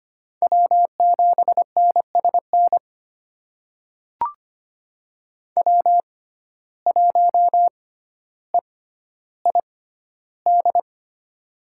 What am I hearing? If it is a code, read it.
Morse code, heard as a keyed tone, about 25 wpm, spaced as for 13 wpm, 700 Hz. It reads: W7NSN W1EID
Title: USA Call Signs - 5 Characters - 6x Character Spacing - 25wpm